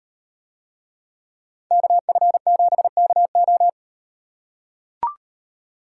Morse code, heard as a keyed tone, about 38 wpm, 700 Hz. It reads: KF7KO